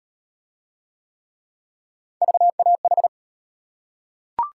Morse code, heard as a keyed tone, about 33 wpm, 700 Hz. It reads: VAH